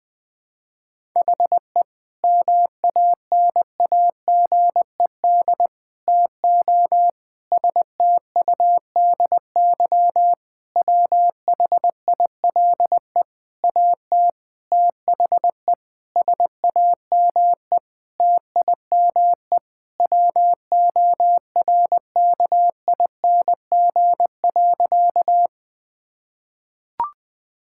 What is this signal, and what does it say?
Morse code, heard as a keyed tone, about 20 wpm, 700 Hz. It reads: HE MANAGED TO STUDY WHILE AT THE SAME TIME WORKING.
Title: Sentences with 5-Word Phrases - Spoken Beginning - 20wpm